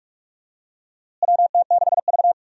Morse code, heard as a keyed tone, about 46 wpm, 700 Hz. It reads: WT6V
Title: USA Call Signs - 4 Characters - Repeated 3x - 45wpm